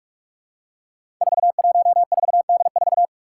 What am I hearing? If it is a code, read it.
Morse code, heard as a keyed tone, about 46 wpm, 700 Hz. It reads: 414D4
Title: Five Challenging Letter-Number Combinations 45wpm